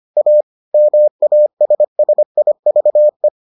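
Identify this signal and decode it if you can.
Morse code, heard as a keyed tone, about 25 wpm, 600 Hz. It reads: A MASSIVE
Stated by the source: Morse edition of the Foundations of Amateur Radio episodes.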